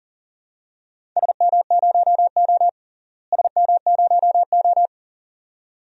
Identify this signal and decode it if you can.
Morse code, heard as a keyed tone, about 37 wpm, 700 Hz. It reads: SM0O SM0O
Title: DXCC-100 Call Signs - 4 Characters - Repeated 2x - 40wpm